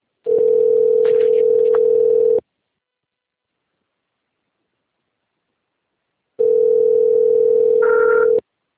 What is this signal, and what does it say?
Ringback tone (North America), 440 + 480 Hz.